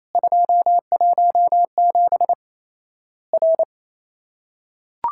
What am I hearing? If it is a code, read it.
Morse code, heard as a keyed tone, about 28 wpm, 700 Hz. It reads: 217 R
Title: Worked All Europe DX Contest - QTC Exchange - 28wpm